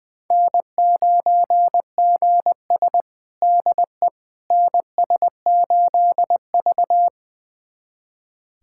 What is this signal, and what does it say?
Morse code, heard as a keyed tone, about 20 wpm, 700 Hz. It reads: N9GS DE NS8V
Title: USA Callsign DE Callsign - 4 Characters - 20wpm